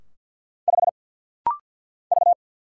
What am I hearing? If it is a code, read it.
Morse code, heard as a keyed tone, about 49 wpm, 700 Hz. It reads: 5 V
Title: Single Letter-Number - Rapid-Fire 50wpm